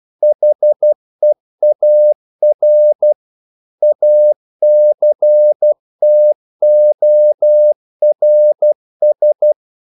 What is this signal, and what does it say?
Morse code, heard as a keyed tone, about 12 wpm, 600 Hz. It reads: HEAR ACTORS
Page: English quotations in CW (telegraphy - morse)